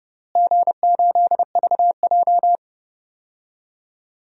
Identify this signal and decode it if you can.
Morse code, heard as a keyed tone, about 32 wpm, 700 Hz. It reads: G8VJ
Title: DXCC-100 Call Signs - 4 Characters - Repeated 3x - 30wpm